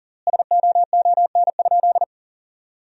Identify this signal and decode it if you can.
Morse code, heard as a keyed tone, about 40 wpm, 700 Hz. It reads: SOON?
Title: 3x Word-Spacing - Sentences from Top 400 Words - 40wpm